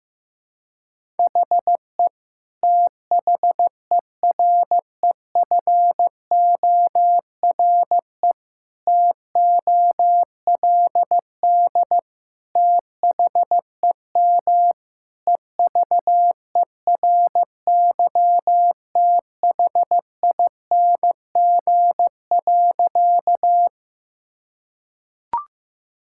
Morse code, heard as a keyed tone, about 15 wpm, 700 Hz. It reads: HE THEREFORE TOLD THEM EVERYTHING.
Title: Sentences from Top 1000 Words - Review Part 5 - 15wpm